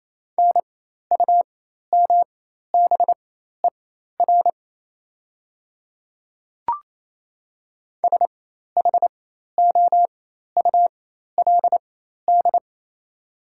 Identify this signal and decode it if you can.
Morse code, heard as a keyed tone, about 28 wpm, spaced as for 13 wpm, 700 Hz. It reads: NUMBER SHOULD
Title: 4x Character Spacing - Top 300 Words - 28wpm